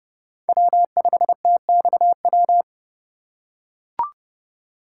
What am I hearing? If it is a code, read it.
Morse code, heard as a keyed tone, about 30 wpm, 700 Hz. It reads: W5TXW
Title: USA Call Signs - 5 Characters - 3x Character Spacing - 30wpm